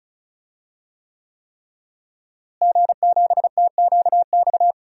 Morse code, heard as a keyed tone, about 36 wpm, 700 Hz. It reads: G7TQX
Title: DXCC-100 Call Signs - 5 Characters - Repeated 3x - 35wpm